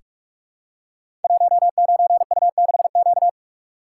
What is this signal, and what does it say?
Morse code, heard as a keyed tone, about 44 wpm, 700 Hz. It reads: 19U6Q
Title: Code Groups - 5 Letter:Numbers - 45wpm